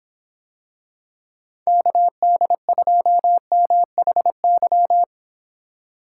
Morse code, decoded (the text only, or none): KD2MHY